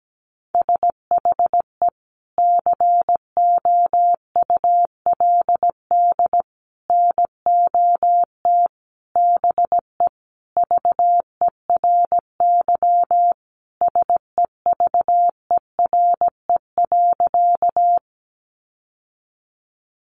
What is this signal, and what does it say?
Morse code, heard as a keyed tone, about 17 wpm, 700 Hz. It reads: SHE COULD NOT BE VERY SEVERE.
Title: Sentences from Top 1400 Words 17wpm